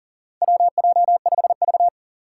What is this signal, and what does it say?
Morse code, heard as a keyed tone, about 38 wpm, 700 Hz. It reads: WJ5V
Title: Four Challenging Letter-Number Combinations 40wpm